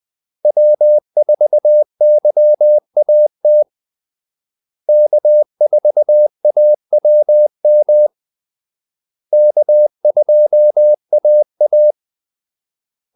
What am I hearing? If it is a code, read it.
Morse code, heard as a keyed tone, about 20 wpm, 600 Hz. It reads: W4YAT K4AWM K2AA
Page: Kaldesignaler 20 wpm